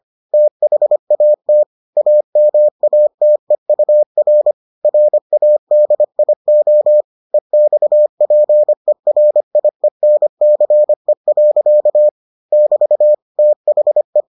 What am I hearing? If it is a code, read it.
Morse code, heard as a keyed tone, about 25 wpm, 600 Hz. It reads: THAT AMATEUR RADIO EXPERIENCE. <BT> THE